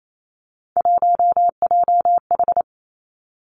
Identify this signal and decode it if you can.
Morse code, heard as a keyed tone, about 28 wpm, 700 Hz. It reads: 1JH
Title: Three Challenging Letter-Number Combinations 28wpm